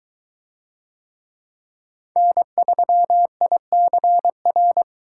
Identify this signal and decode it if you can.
Morse code, heard as a keyed tone, about 21 wpm, 700 Hz. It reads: N3ICR